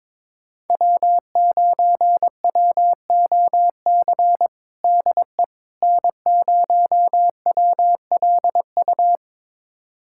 Morse code, heard as a keyed tone, about 22 wpm, 700 Hz. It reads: W9WOC DE N0WLU
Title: USA Callsign DE Callsign - 5 Characters - 22wpm